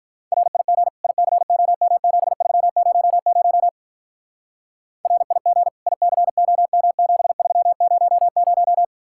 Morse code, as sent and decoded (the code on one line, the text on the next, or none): .-. .. --. / .. -.-. --- -- --... ...-- ----- ----- / .-. .. --. / .. -.-. --- -- --... ...-- ----- -----
RIG ICOM7300 RIG ICOM7300